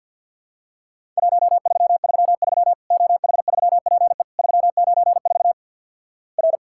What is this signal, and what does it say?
Morse code, heard as a keyed tone, about 50 wpm, 700 Hz. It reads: 1333 OH3PE 394 R
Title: Worked All Europe DX Contest - QTC Exchange with Repeats - 50wpm